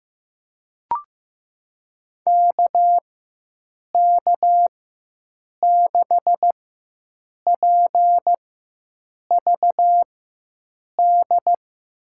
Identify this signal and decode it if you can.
Morse code, heard as a keyed tone, about 16 wpm, spaced as for 7 wpm, 700 Hz. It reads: KK6PVD
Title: USA Call Signs - 6 Characters - 4x Character Spacing - 15wpm